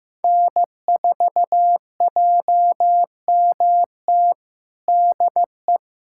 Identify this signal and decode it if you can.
Morse code, heard as a keyed tone, about 15 wpm, 700 Hz. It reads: N4JMT DE
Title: USA Callsign DE Callsign - 5 Characters - 15wpm